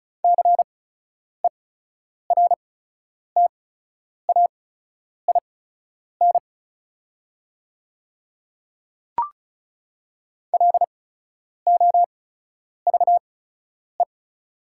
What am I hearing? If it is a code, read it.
Morse code, heard as a keyed tone, about 34 wpm, spaced as for 10 wpm, 700 Hz. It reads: CERTAIN LOVE
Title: 8x Character Spacing - Top 500 Words - 35wpm